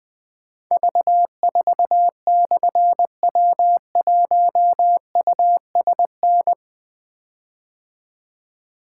Morse code, heard as a keyed tone, about 20 wpm, 700 Hz. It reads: V4/W1USN
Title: Difficult Call Signs 20wpm